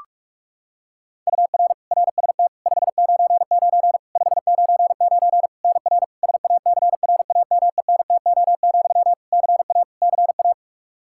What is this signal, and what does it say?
Morse code, heard as a keyed tone, about 46 wpm, 700 Hz. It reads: UR RST 599 599 NR SACRAMENTO, CA CA
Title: K1USN SST QSO Practice 45wpm